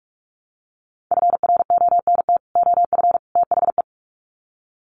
Morse code, heard as a keyed tone, about 44 wpm, 700 Hz. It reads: FRONT OF THE
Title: Sets of 3 Words 45wpm